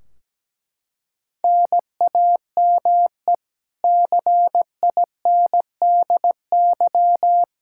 Morse code, heard as a keyed tone, about 17 wpm, 700 Hz. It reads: NAME CINDY
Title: K1USN SST QSO Practice 17wpm